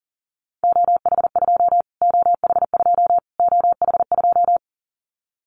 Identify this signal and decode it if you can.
Morse code, heard as a keyed tone, about 40 wpm, 700 Hz. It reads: OH2 OH2 OH2
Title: Callsign Prefixes for Top 25 DXCC - Repeated 3x - 40wpm